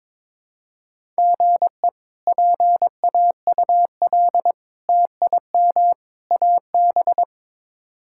Morse code, heard as a keyed tone, about 22 wpm, 700 Hz. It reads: GE PAUL TIM AB
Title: K1USN SST QSO Practice - 22wpm